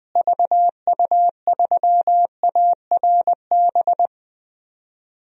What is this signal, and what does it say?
Morse code, heard as a keyed tone, about 20 wpm, 700 Hz. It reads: VU3ARB
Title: DXCC-100 Call Signs - 6 Characters - Repeated 2x - 20wpm